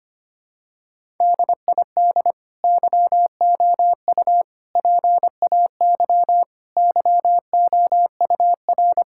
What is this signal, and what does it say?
Morse code, heard as a keyed tone, about 25 wpm, 700 Hz. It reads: DID YOU PAY YOUR
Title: Sentences from Top 1000 Words - Review Part 5 - 25wpm